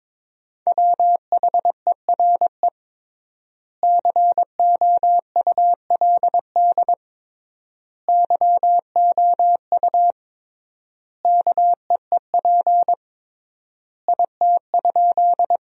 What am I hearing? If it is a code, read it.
Morse code, heard as a keyed tone, about 22 wpm, 700 Hz. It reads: WHERE COULD YOU KEEP IT?
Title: 3x Word-Spacing - Sentences from Top 300 Words - 22wpm